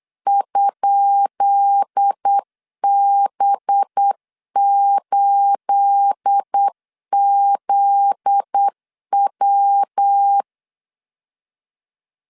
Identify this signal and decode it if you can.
Morse code, heard as a keyed tone, about 8 wpm, 800 Hz. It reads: ?B8ZW
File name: LEZIONI / 10 SIGNS / 10B INT 8wpm 3534